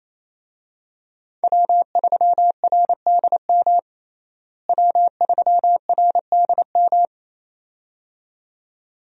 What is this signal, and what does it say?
Morse code, heard as a keyed tone, about 27 wpm, 700 Hz. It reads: W3RDM W3RDM
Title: USA Call Signs - 5 Characters - Repeated 2x - 28wpm